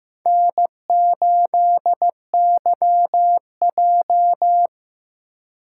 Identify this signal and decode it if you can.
Morse code, heard as a keyed tone, about 15 wpm, 700 Hz. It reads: N8YJ
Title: USA Call Signs - 4x Character Spacing - 15wpm